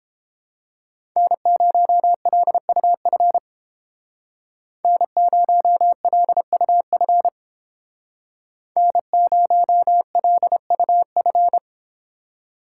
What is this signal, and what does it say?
Morse code, heard as a keyed tone, about 29 wpm, 700 Hz. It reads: N0LUF N0LUF N0LUF